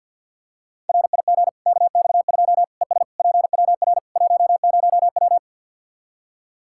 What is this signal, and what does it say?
Morse code, heard as a keyed tone, about 53 wpm, 700 Hz. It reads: RIG KX2 ES PWR 10W
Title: K1USN SST QSO Practice 50wpm